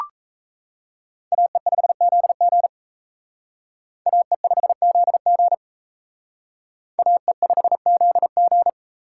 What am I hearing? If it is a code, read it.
Morse code, heard as a keyed tone, about 36 wpm, 700 Hz. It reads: AE5ZG AE5ZG AE5ZG